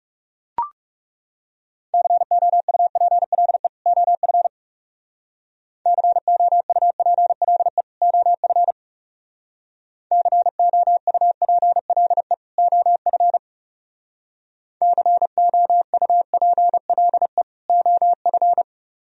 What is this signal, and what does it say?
Morse code, heard as a keyed tone, about 36 wpm, 700 Hz. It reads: COUPLE OF COUPLE OF COUPLE OF COUPLE OF
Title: Speed-Racing - Sets of 2 Words 30wpm